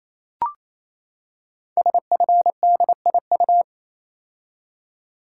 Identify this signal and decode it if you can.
Morse code, heard as a keyed tone, about 28 wpm, 700 Hz. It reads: SFDIU
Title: Code Groups - 5 Letters - 28wpm